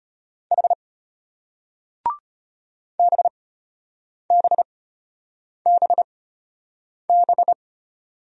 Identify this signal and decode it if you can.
Morse code, heard as a keyed tone, about 33 wpm, spaced as for 8 wpm, 700 Hz. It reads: H BBBB